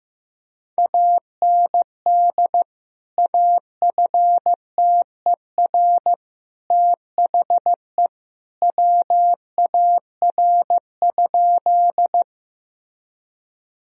Morse code, heard as a keyed tone, about 15 wpm, 700 Hz. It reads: AND AFTER THE WAR?
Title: Sentences from Top 1000 Words - Review Part 2 - 15wpm